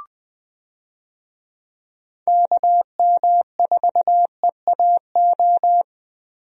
Morse code, heard as a keyed tone, about 21 wpm, 700 Hz. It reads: KM4EAO
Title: USA Call Signs - 6 Characters - Repeated 3x - 20wpm